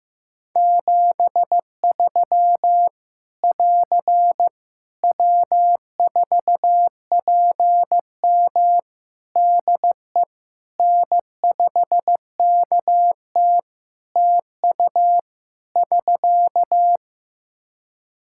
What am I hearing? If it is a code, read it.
Morse code, heard as a keyed tone, about 15 wpm, 700 Hz. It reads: 73 <AR> W4PM DE N5KT TU <SK>
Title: K1USN SST QSO Practice 15wpm